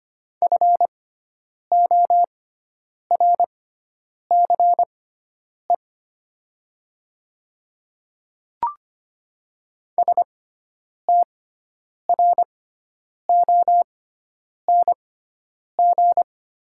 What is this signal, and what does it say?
Morse code, heard as a keyed tone, about 25 wpm, spaced as for 9 wpm, 700 Hz. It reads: FORCE STRONG
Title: 6x Character Spacing - Top 500 Words - 25wpm